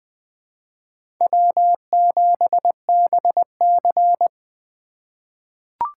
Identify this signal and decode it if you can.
Morse code, heard as a keyed tone, about 20 wpm, 700 Hz. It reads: W7BC
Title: USA Call Signs - 4 Characters - 4x Character Spacing - 20wpm